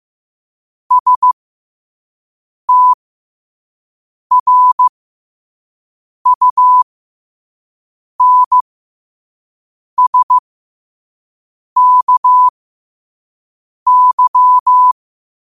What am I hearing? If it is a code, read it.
Morse code, heard as a keyed tone, about 15 wpm, spaced as for 5 wpm, 1000 Hz. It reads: STRUNSKY
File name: qotdfiles / 5wpm